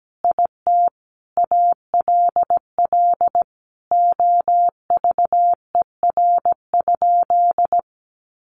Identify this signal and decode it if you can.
Morse code, heard as a keyed tone, about 17 wpm, 700 Hz. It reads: IT ALL OVER?
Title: Sentences from Top 1200 Words 17wpm